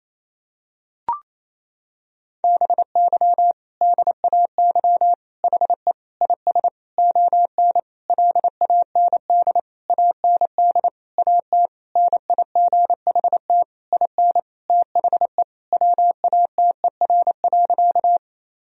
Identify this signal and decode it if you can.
Morse code, heard as a keyed tone, about 28 wpm, 700 Hz. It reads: BY DAY HE IS ON LAND AND AT NIGHT IN THE WATER.
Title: Sentences from Top 1000 Words - Review Part 1 - 28wpm